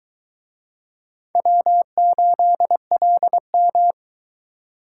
Morse code, heard as a keyed tone, about 20 wpm, 700 Hz. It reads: W8LM